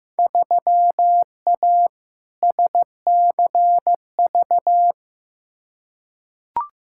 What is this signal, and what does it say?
Morse code, heard as a keyed tone, about 15 wpm, 700 Hz. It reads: 3A SCV
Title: Field Day Exchanges - 15wpm